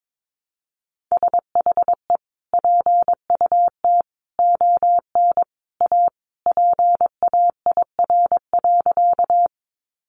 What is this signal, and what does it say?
Morse code, heard as a keyed tone, about 22 wpm, 700 Hz. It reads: SHE PUT ON A PAIR.
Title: Sentences from Top 1200 Words 22wpm